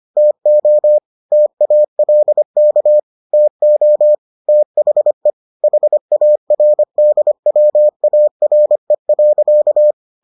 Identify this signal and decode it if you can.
Morse code, heard as a keyed tone, about 25 wpm, 600 Hz. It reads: TO TALK TO THE HARDWARE.